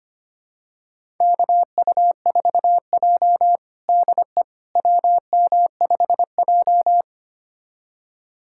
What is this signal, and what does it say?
Morse code, heard as a keyed tone, about 25 wpm, 700 Hz. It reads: KU4J DE WM5J
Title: USA Callsign DE Callsign - 4 Characters - 25wpm